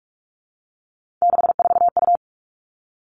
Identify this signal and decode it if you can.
Morse code, heard as a keyed tone, about 46 wpm, 700 Hz. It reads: 64U
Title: Three Challenging Letter-Number Combinations 45wpm